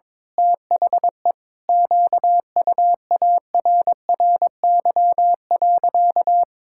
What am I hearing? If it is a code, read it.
Morse code, heard as a keyed tone, about 22 wpm, 700 Hz. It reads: THE QUARRY.
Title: Pangram Challenge v2 - 22wpm